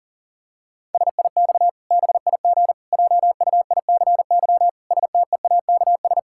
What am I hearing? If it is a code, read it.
Morse code, heard as a keyed tone, about 40 wpm, 700 Hz. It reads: SIX BIG JUICY STEAKS
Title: Pangram Challenge v2 - 40wpm